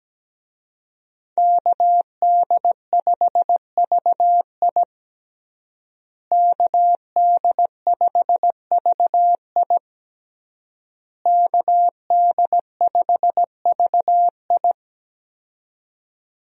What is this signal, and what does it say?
Morse code, heard as a keyed tone, about 17 wpm, 700 Hz. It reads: KD5VI KD5VI KD5VI
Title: USA Call Signs - 5 Characters - Repeated 3x - 17wpm